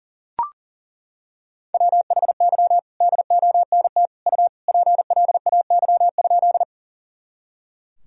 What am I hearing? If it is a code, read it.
Morse code, heard as a keyed tone, about 40 wpm, 700 Hz. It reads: WHY DONT U PLAY?
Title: Sentences from Top 300 Words and Abbreviations 40wpm